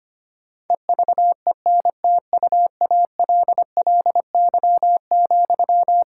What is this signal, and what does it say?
Morse code, heard as a keyed tone, about 25 wpm, 700 Hz. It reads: EVENTUALLY,
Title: Sentences from Top 1000 Words - Review Part 6 - 25wpm